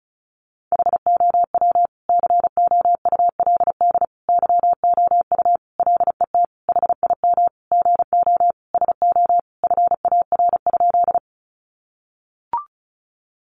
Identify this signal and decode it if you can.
Morse code, heard as a keyed tone, about 34 wpm, 700 Hz. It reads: HOW COULD YOU LET HIM GO SO FAR?
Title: Sentences from Top 300 Words 35wpm